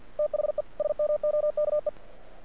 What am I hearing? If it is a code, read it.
Morse code, heard as a keyed tone, about 48 wpm, 600 Hz. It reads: THE SMOKE
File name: archived / CRSnet